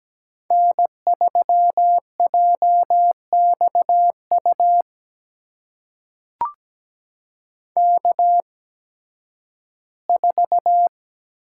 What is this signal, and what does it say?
Morse code, heard as a keyed tone, about 17 wpm, spaced as for 11 wpm, 700 Hz. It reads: N3JXU K4
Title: USA Call Signs - 8x Character Spacing - 17wpm